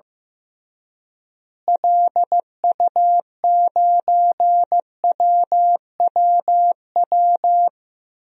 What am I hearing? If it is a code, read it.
Morse code, heard as a keyed tone, about 15 wpm, 700 Hz. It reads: LU9WWW
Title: DXCC-100 Call Signs - 6 Characters - Repeated 2x - 15wpm